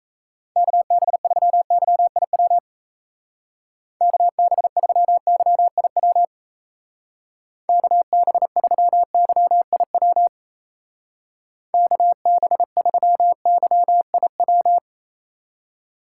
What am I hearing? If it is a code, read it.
Morse code, heard as a keyed tone, about 34 wpm, 700 Hz. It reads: KB3YIW KB3YIW KB3YIW KB3YIW
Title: Speed Racing - USA 2x3 Call Signs 28wpm